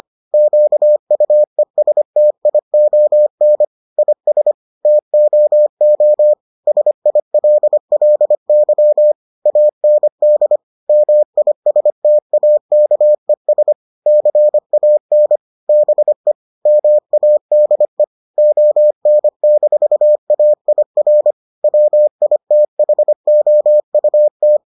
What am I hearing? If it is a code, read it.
Morse code, heard as a keyed tone, about 25 wpm, 600 Hz. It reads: QUESTION IS TOO SILLY AND MISTAKES CAN BE MADE ON-AIR WITHOUT